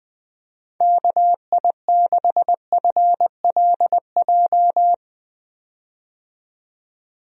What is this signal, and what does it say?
Morse code, heard as a keyed tone, about 20 wpm, 700 Hz. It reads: KI6FLJ
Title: USA Call Signs - 6 Characters - 20wpm